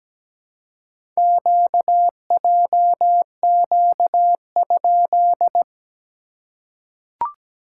Morse code, heard as a keyed tone, about 17 wpm, 700 Hz. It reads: QJQ?
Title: SOTA QSO Practice 17wpm